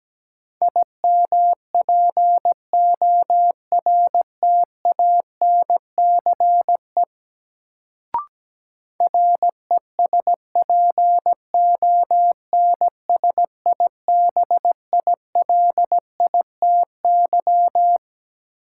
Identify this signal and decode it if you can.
Morse code, heard as a keyed tone, about 17 wpm, 700 Hz. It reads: IMPORTANCE RESPONSIBILITY
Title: Sentences from Top 1200 Words - New Word Drill - 17wpm